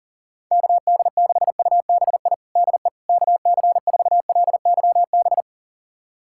Morse code, heard as a keyed tone, about 37 wpm, 700 Hz. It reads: KD6UBI DE KC4LYB
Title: USA Callsign DE Callsign - 6 Characters - 40wpm